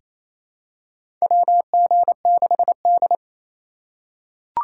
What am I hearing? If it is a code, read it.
Morse code, heard as a keyed tone, about 28 wpm, 700 Hz. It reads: WG6D
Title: USA Call Signs - 4 Characters - 2x Character Spacing - 28wpm